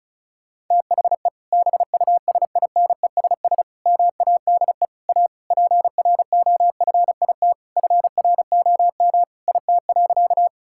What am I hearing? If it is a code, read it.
Morse code, heard as a keyed tone, about 36 wpm, 700 Hz. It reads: THE BUSINESS MADE A PROFIT FROM IT.
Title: Sentences from Top 1000 Words - Review Part 5 - 35wpm